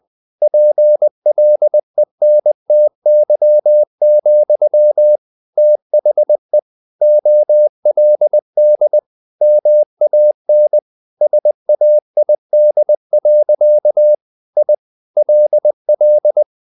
Morse code, heard as a keyed tone, about 20 wpm, 600 Hz. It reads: PLENTY, THE OLD MAN SAID. I LL
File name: the old man and the sea part 2 20wpm